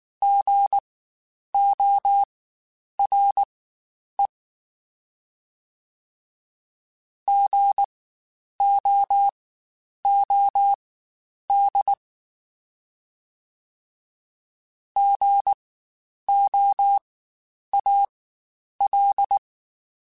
Morse code, heard as a keyed tone, about 19 wpm, spaced as for 9 wpm, 800 Hz. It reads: GORE GOOD GOAL
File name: Spaced Alphas / Spaced Alphas mp3s / Gs sp @18